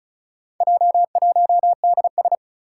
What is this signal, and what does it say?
Morse code, heard as a keyed tone, about 36 wpm, 690 Hz. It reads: J1DS
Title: Four Challenging Letter-Number Combinations 35wpm